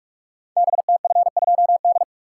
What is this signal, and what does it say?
Morse code, heard as a keyed tone, about 44 wpm, 700 Hz. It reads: BTF2D
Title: Code Groups - 5 Letter:Numbers - 45wpm